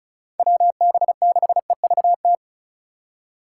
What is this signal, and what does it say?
Morse code, heard as a keyed tone, about 47 wpm, 700 Hz. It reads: WB6EVT